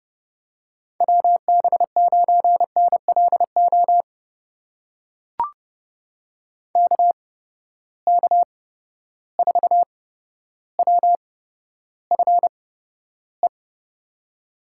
Morse code, heard as a keyed tone, about 30 wpm, spaced as for 13 wpm, 700 Hz. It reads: WB9NLO KK4WFE
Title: USA Call Signs - 6 Characters - 8x Character Spacing - 30wpm